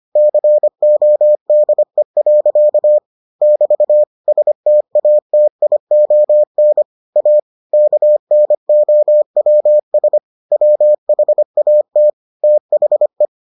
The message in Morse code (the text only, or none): CODE. <BT> STATION A KNOWS WHAT THE